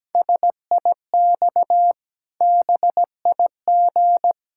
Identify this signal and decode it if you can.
Morse code, heard as a keyed tone, about 17 wpm, 700 Hz. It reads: SIX BIG
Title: Pangram Challenge v2 - 17wpm